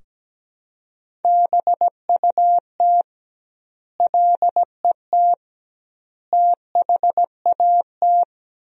Morse code, heard as a keyed tone, about 17 wpm, 700 Hz. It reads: BUT LET THAT
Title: 2x Word-Spacing - Sentences from Top 400 Words - 17wpm